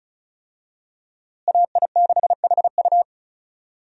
Morse code, heard as a keyed tone, about 34 wpm, 700 Hz. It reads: AI6HU
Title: USA Call Signs - 5 Characters - Repeated 2x - 35wpm